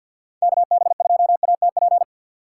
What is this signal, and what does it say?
Morse code, heard as a keyed tone, about 48 wpm, 700 Hz. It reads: KB2ATP
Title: USA Call Signs - 6 Characters - 6x Character Spacing - 50wpm